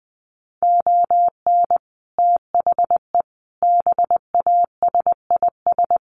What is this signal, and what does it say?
Morse code, heard as a keyed tone, about 20 wpm, 700 Hz. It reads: ON THE BASIS
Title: Sets of 5 Words Encore 20wpm